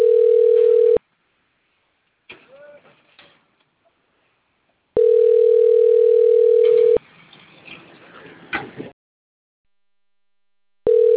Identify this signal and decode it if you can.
Ringback tone (North America), 440 + 480 Hz.